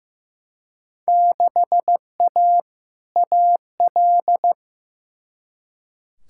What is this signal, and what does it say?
Morse code, heard as a keyed tone, about 15 wpm, 700 Hz. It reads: 6A AL
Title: Field Day Exchanges Encore - 15wpm